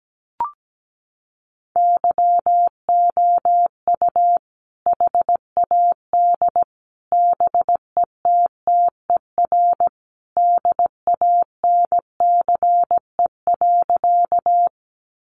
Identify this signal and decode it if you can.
Morse code, heard as a keyed tone, about 17 wpm, 700 Hz. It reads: YOU HAD BETTER DANCE.
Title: Sentences from Top 700 Words 17wpm